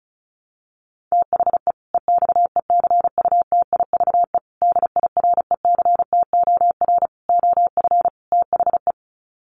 Morse code, heard as a keyed tone, about 34 wpm, 700 Hz. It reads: THE EXECUTIVE DIRECTOR OF THE
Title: Sets of 5 Words Encore 35wpm